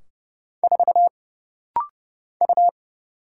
Morse code, heard as a keyed tone, about 30 wpm, 700 Hz. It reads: 4 U
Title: Single Letter-Number Alt - Rapid-Fire 30wpm